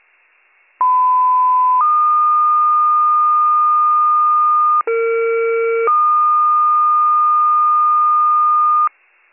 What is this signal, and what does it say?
Two-tone page, heard as a one-second tone, then a three-second tone, about 1010 then 1220 Hz, about 460 then 1220 Hz. It is heard from 0.8 to 8.9 s.